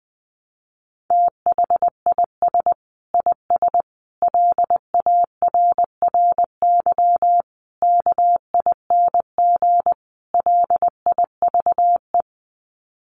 Morse code, heard as a keyed tone, about 20 wpm, 700 Hz. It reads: THIS IS LARRY KING LIVE
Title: Sets of 5 Words Encore2 20wpm